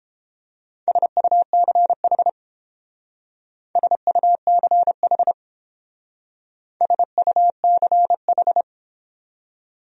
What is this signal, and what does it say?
Morse code, heard as a keyed tone, about 31 wpm, 700 Hz. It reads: SUCH SUCH SUCH